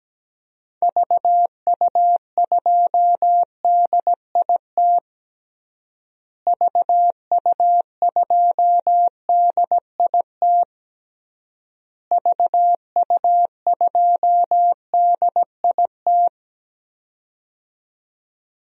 Morse code, heard as a keyed tone, about 17 wpm, 700 Hz. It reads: VU2DIT VU2DIT VU2DIT
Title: DXCC-100 Call Signs - 6 Characters - Repeated 3x - 17wpm